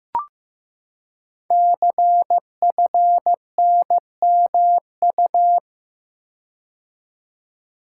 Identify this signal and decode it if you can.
Morse code, heard as a keyed tone, about 15 wpm, 700 Hz. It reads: CFNMU
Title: Code Groups - 5 Letters - 15wpm